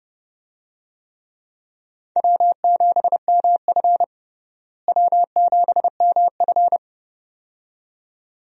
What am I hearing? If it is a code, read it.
Morse code, heard as a keyed tone, about 30 wpm, 700 Hz. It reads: W7MF W7MF
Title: USA Call Signs - 4 Characters - Repeated 2x - 30wpm